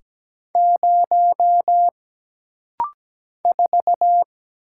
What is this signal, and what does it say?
Morse code, heard as a keyed tone, about 17 wpm, 700 Hz. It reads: 0 4
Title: Single Letter-Number - Rapid-Fire 17wpm